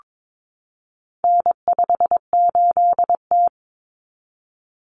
Morse code, heard as a keyed tone, about 22 wpm, 700 Hz. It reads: N58T
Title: Common Cut Numbers - 4-Digit 22wpm